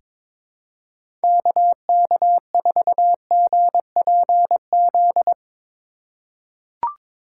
Morse code, heard as a keyed tone, about 22 wpm, 700 Hz. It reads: KK4GPZ
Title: USA Call Signs - 6 Characters - 2x Character Spacing - 22wpm